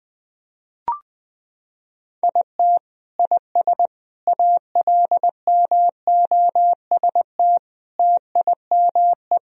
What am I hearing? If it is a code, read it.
Morse code, heard as a keyed tone, about 20 wpm, 700 Hz. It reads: IT IS ALMOST TIME
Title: Sentences from Top 1000 Words - Review Part 4 - 20wpm